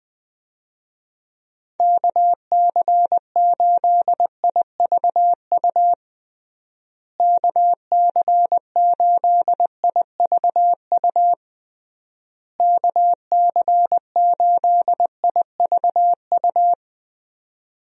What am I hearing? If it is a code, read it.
Morse code, heard as a keyed tone, about 20 wpm, 700 Hz. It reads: KC8IVU KC8IVU KC8IVU
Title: USA Call Signs - 6 Characters - Repeated 3x - 20wpm